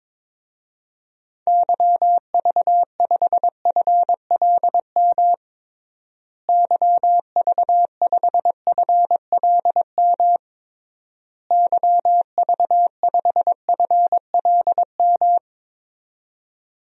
Morse code, heard as a keyed tone, about 21 wpm, 700 Hz. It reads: YV5FLM YV5FLM YV5FLM
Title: DXCC-100 Call Signs - 6 Characters - Repeated 3x - 22wpm